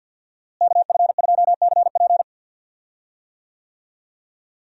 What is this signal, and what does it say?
Morse code, heard as a keyed tone, about 46 wpm, 700 Hz. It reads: KF2CP
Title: USA Call Signs - 5 Characters - Repeated 3x - 50wpm